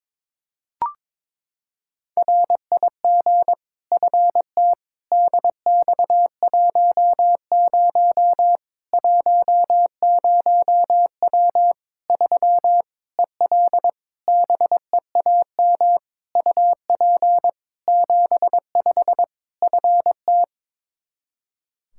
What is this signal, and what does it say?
Morse code, heard as a keyed tone, about 22 wpm, 700 Hz. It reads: RIG FT DX10 10W 3 EL BEAM UP 75 FT
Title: K1USN SST QSO Practice 22wpm